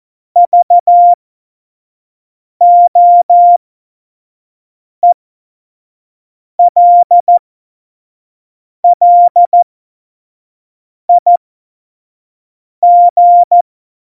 Morse code, heard as a keyed tone, about 14 wpm, spaced as for 5 wpm, 700 Hz. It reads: VOELLIG
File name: Morsekurs / Witze / Witz 27 Juni 17, Tempo 25 F70